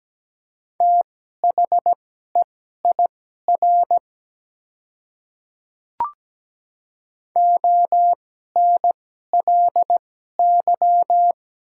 Morse code, heard as a keyed tone, about 17 wpm, spaced as for 12 wpm, 700 Hz. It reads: THEIR ONLY
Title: 2x Character Spacing - Top 300 Words - 17wpm